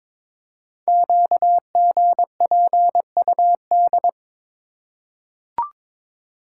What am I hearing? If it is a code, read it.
Morse code, heard as a keyed tone, about 22 wpm, 700 Hz. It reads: QGPUD
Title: Code Groups - 5 Letters - 22wpm